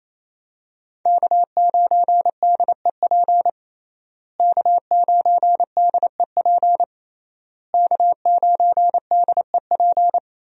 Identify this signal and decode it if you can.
Morse code, heard as a keyed tone, about 28 wpm, 700 Hz. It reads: K9DEP K9DEP K9DEP
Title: USA Call Signs - 5 Characters - Repeated 3x - 28wpm